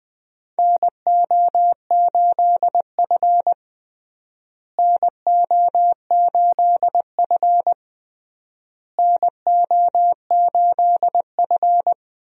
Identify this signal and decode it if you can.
Morse code, heard as a keyed tone, about 20 wpm, 700 Hz. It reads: NO8F NO8F NO8F
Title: USA Call Signs - 4 Characters - Repeated 3x - 20wpm